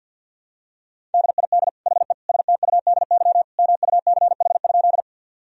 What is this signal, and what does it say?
Morse code, heard as a keyed tone, about 50 wpm, 700 Hz. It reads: DID HE STUDY MUCH?
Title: Sentences from Top 1000 Words - Review Part 1 - 50wpm